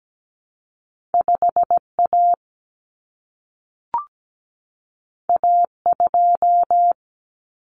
Morse code, heard as a keyed tone, about 17 wpm, 700 Hz. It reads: 5A A2
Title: Common Cut Numbers - 2-Digit 17wpm